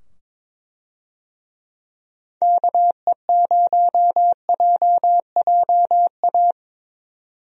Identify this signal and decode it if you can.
Morse code, heard as a keyed tone, about 22 wpm, 700 Hz. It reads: KE0JJA
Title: USA Call Signs - 6 Characters - 22wpm